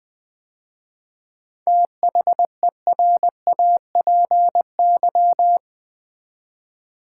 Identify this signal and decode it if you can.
Morse code, heard as a keyed tone, about 20 wpm, 700 Hz. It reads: THERAPY
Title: 7 Letter Words - First 3 Letters Given - 20wpm